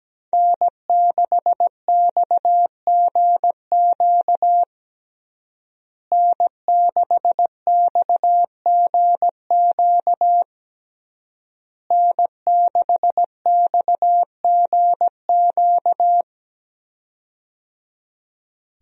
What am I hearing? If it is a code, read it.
Morse code, heard as a keyed tone, about 17 wpm, 700 Hz. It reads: N6XGQ N6XGQ N6XGQ
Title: USA Call Signs - 5 Characters - Repeated 3x - 17wpm